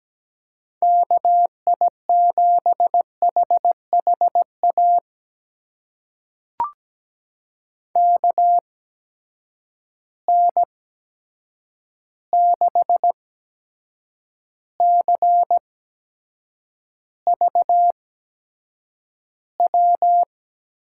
Morse code, heard as a keyed tone, about 17 wpm, spaced as for 7 wpm, 700 Hz. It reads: KI7HHA KN6CVW
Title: USA Call Signs - 6 Characters - 8x Character Spacing - 17wpm